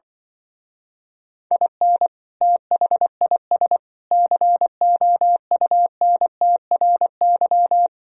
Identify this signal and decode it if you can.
Morse code, heard as a keyed tone, about 25 wpm, 700 Hz. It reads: IN THIS COUNTRY
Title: Speed-Racing - Sets of 3 Words 20wpm